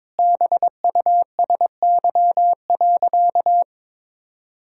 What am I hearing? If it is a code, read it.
Morse code, heard as a keyed tone, about 22 wpm, 700 Hz. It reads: BUSY.
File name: Sentences with 7-Word Phrases - 22wpm